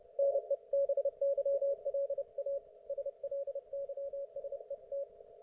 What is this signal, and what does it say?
Morse code, heard as a keyed tone, about 29 wpm, 560 Hz. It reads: NEBYLA SLYSET